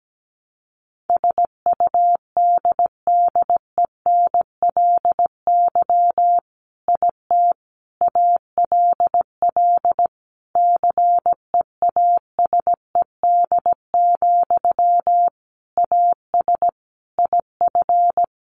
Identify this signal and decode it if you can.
Morse code, heard as a keyed tone, about 17 wpm, 700 Hz. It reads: SUDDENLY IT ALL CEASED, AS IF
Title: A Sound Of Thunder 17wpm